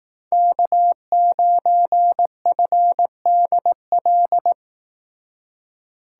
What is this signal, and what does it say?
Morse code, heard as a keyed tone, about 18 wpm, 700 Hz. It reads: K9FDL
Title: Speed Racing - USA 1x3 Call Signs 15wpm